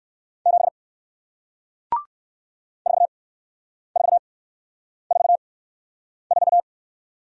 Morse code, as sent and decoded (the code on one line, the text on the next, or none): --... / ....- ....- ....- ....-
7 4444